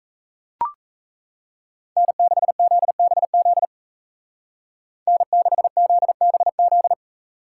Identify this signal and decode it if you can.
Morse code, heard as a keyed tone, about 40 wpm, 700 Hz. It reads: N6ZBZ N6ZBZ